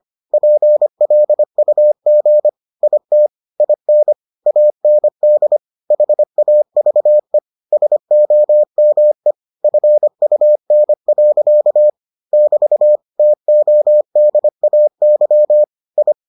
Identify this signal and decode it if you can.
Morse code, heard as a keyed tone, about 25 wpm, 600 Hz. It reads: PLUG IT IN AND HAVE SOME FUN. <BT> TODAY I